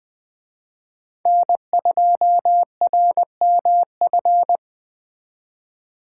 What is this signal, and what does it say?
Morse code, heard as a keyed tone, about 21 wpm, 700 Hz. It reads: N2RMF